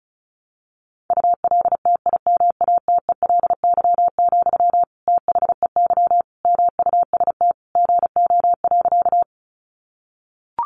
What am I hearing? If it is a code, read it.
Morse code, heard as a keyed tone, about 34 wpm, 700 Hz. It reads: ULTIMATELY, THEY MUST GO.
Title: Sentences from Top 1200 Words 35wpm